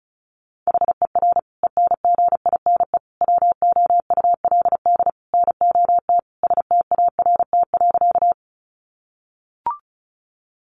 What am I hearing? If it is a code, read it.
Morse code, heard as a keyed tone, about 34 wpm, 700 Hz. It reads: HER ENGINE WOULD NOT START.
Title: Sentences from Top 1500 Words 35wpm